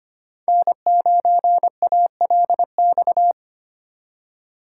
Morse code, heard as a keyed tone, about 25 wpm, 700 Hz. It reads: N9ALX
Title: USA Call Signs - 5 Characters - 3x Character Spacing - 25wpm